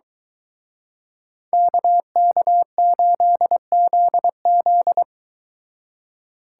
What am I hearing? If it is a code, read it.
Morse code, heard as a keyed tone, about 23 wpm, 700 Hz. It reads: KK8ZZ